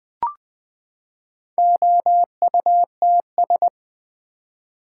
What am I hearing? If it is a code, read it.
Morse code, heard as a keyed tone, about 20 wpm, 700 Hz. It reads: OUTS
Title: Build A Word - 4 Letters At A Time - 11-Letter Words - 20wpm